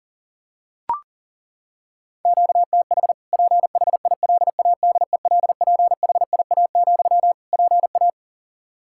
Morse code, heard as a keyed tone, about 37 wpm, 700 Hz. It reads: QTH PHILADELPHIA, PA
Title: Practice QSO - Exchange 1 - Beginning - 40wpm